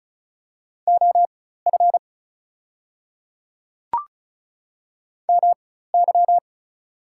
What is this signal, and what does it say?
Morse code, heard as a keyed tone, about 36 wpm, spaced as for 16 wpm, 700 Hz. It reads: OF MY
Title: 4x Character Spacing - Top 300 Words - 35wpm